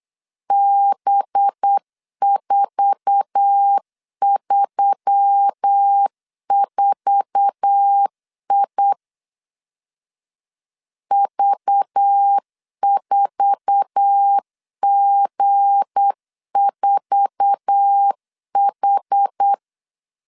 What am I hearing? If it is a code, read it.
Morse code, heard as a keyed tone, about 8 wpm, 800 Hz. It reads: B434I V4G4H
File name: LEZIONI / 8 YJ45 / 8 YJ4 8wpm 3246